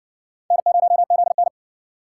Morse code, heard as a keyed tone, about 57 wpm, 700 Hz. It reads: N0ZN